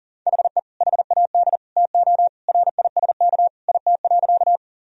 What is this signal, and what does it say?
Morse code, heard as a keyed tone, about 40 wpm, 700 Hz. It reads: HE HAD TO RISK IT.
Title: Sentences from Top 1000 Words - Review Part 4 - 40wpm